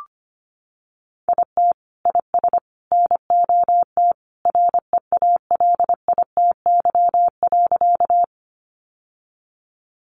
Morse code, heard as a keyed tone, about 25 wpm, 700 Hz. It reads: IT IS NOT REALITY.
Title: Sentences from Top 1200 Words 25wpm